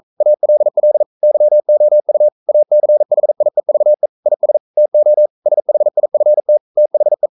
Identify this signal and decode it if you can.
Morse code, heard as a keyed tone, about 42 wpm, 600 Hz. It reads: ALL YOU ACHIEVE IS TO SHIFT THE
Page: Romanen Never af Ken Follet på engelsk.